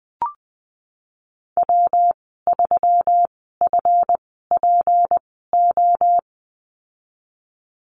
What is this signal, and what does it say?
Morse code, heard as a keyed tone, about 20 wpm, spaced as for 15 wpm, 700 Hz. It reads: W3FPO
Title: 186 Lesson 30g - Letter b - Call Signs 20wpm